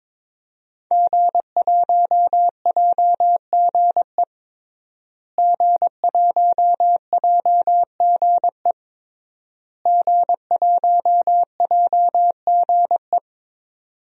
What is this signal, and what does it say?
Morse code, heard as a keyed tone, about 22 wpm, 700 Hz. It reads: G1JGE G1JGE G1JGE
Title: DXCC-100 Call Signs - 5 Characters - Repeated 3x - 22wpm